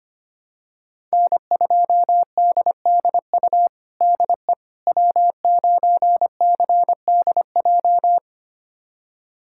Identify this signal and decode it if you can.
Morse code, heard as a keyed tone, about 25 wpm, 700 Hz. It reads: N2DDU DE W9CDJ
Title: USA Callsign DE Callsign - 5 Characters - 25wpm